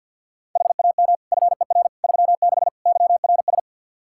Morse code, heard as a keyed tone, about 46 wpm, 700 Hz. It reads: HAM FER 36 YRS
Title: K1USN SST QSO Practice 50wpm